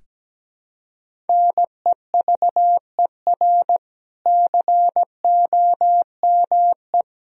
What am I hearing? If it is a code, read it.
Morse code, heard as a keyed tone, about 17 wpm, 700 Hz. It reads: NEVER COME
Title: Sentences from Top 1000 Words - Review Part 1 - 17wpm